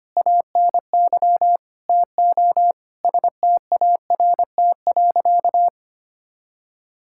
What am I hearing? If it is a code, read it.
Morse code, heard as a keyed tone, about 25 wpm, 700 Hz. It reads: ANY TO START.
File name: practice / 25wpm / Sentences with 9-Word Phrases - 25wpm